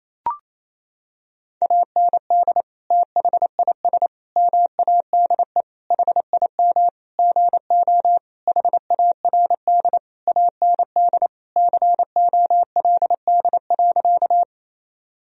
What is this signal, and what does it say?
Morse code, heard as a keyed tone, about 28 wpm, 700 Hz. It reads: AND THIS MADE HIM GO HARD AND COLD.
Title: Sentences from Top 1000 Words - Review Part 1 - 28wpm